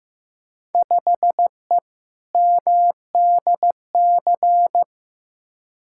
Morse code, heard as a keyed tone, about 15 wpm, 700 Hz. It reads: 5E MDC
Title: Field Day Exchanges - 15wpm